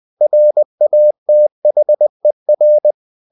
Morse code, heard as a keyed tone, about 20 wpm, 600 Hz. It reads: RATHER
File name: cw-books / winnie-the-pooh / winnie-20wpm-page-0003